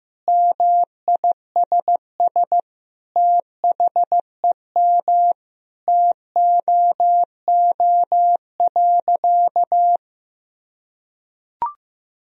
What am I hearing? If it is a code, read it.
Morse code, heard as a keyed tone, about 16 wpm, 700 Hz. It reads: MISS THEM TOO.
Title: Sentences from Top 1000 Words - Review Part 3 - 15wpm